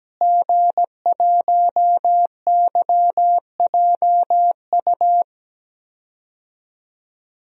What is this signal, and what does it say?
Morse code, heard as a keyed tone, about 17 wpm, 700 Hz. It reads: G1YJU
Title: DXCC-100 Call Signs - 5 Characters - Repeated 3x - 17wpm